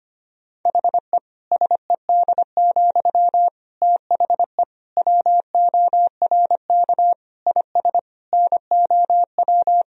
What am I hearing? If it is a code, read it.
Morse code, heard as a keyed tone, about 25 wpm, 700 Hz. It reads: HE SED, THE WORK IS NOW
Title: Sentences from Top 400 Words and Abbreviations 25wpm